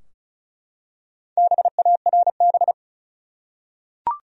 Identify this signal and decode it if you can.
Morse code, heard as a keyed tone, about 35 wpm, 700 Hz. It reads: BARB
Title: USA Common Names - Max 5 Letters - 35wpm